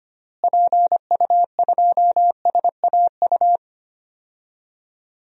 Morse code, heard as a keyed tone, about 25 wpm, 700 Hz. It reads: PU2SAU
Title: DXCC-100 Call Signs - 6 Characters - Repeated 2x - 25wpm